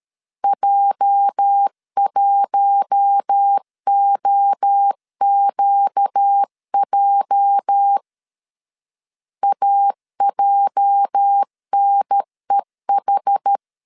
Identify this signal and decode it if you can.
Morse code, heard as a keyed tone, about 13 wpm, 800 Hz. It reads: J1OQJ AJNEH
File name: LEZIONI / 8 YJ45 / 8 YJ 12wpm 2235